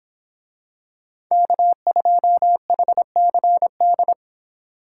Morse code, heard as a keyed tone, about 25 wpm, 700 Hz. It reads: K2HCD